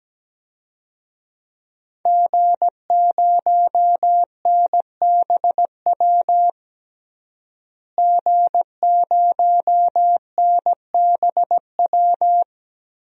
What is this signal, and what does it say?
Morse code, heard as a keyed tone, about 17 wpm, 700 Hz. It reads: G0NBW G0NBW
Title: DXCC-100 Call Signs - 5 Characters - Repeated 3x - 17wpm